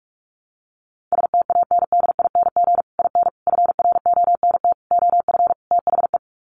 Morse code, heard as a keyed tone, about 44 wpm, 700 Hz. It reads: STANDING IN FRONT OF THE
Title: Sets of 5 Words Encore2 45wpm